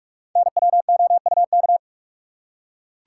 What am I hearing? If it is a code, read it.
Morse code, heard as a keyed tone, about 44 wpm, 700 Hz. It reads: NWOUK
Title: Code Groups - 5 Letters - 45wpm